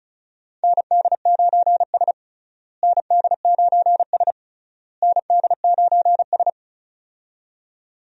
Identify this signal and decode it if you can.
Morse code, heard as a keyed tone, about 36 wpm, 700 Hz. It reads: ND9S ND9S ND9S
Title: USA Call Signs - 4 Characters - Repeated 3x - 35wpm